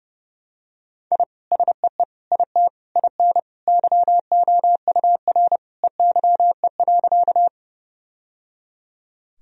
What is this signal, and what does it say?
Morse code, heard as a keyed tone, about 30 wpm, 700 Hz. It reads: I SEE IT IN YOUR EYE.
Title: Sentences from Top 1000 Words - Review Part 1 - 30wpm